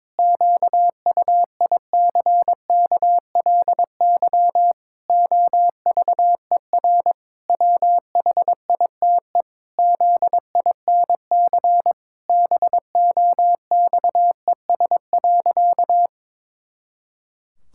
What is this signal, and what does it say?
Morse code, heard as a keyed tone, about 22 wpm, 700 Hz. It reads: QUICKLY OVER WHITE ZINC BOXES.
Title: Pangram Challenge v2 - 22wpm